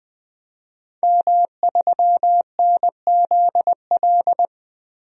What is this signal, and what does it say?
Morse code, heard as a keyed tone, about 20 wpm, 700 Hz. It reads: M3NZL
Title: DXCC-100 Call Signs - 5 Characters - 20wpm